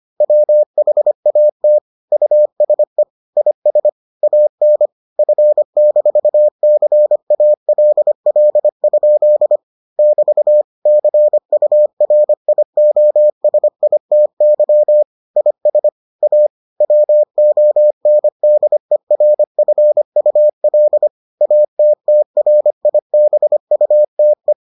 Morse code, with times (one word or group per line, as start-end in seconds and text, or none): 0.2-1.8 WHAT
2.1-3.0 USE
3.4-3.9 IS
4.2-4.9 AN
5.2-9.6 F-CALL?
10.0-10.6 <BT>
10.9-15.0 CURIOSITY
15.4-15.9 IS
16.2-16.5 A
16.8-21.1 WONDERFUL
21.4-24.5 ATTRIBUTE